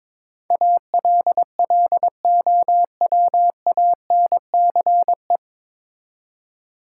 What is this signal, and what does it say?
Morse code, heard as a keyed tone, about 22 wpm, 700 Hz. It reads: ALLOWANCE
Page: Morse Code Ninja practice 22wpm